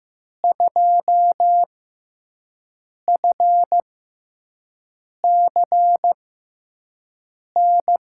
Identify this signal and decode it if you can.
Morse code, heard as a keyed tone, about 15 wpm, spaced as for 5 wpm, 700 Hz. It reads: 2FCN
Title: USA Call Signs - 6 Characters - 6x Character Spacing - 15wpm